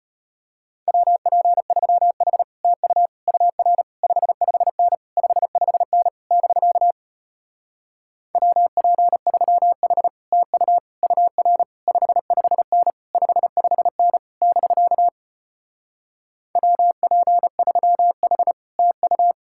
Morse code, heard as a keyed tone, about 33 wpm, 700 Hz. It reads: WP3H TU UR 55N 55N <BK> WP3H TU UR 55N 55N <BK> WP3H TU